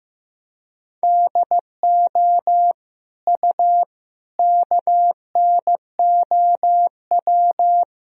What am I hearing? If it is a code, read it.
Morse code, heard as a keyed tone, about 15 wpm, 700 Hz. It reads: DO U KNOW
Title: Sentences from Top 100 Words and Abbreviations 15wpm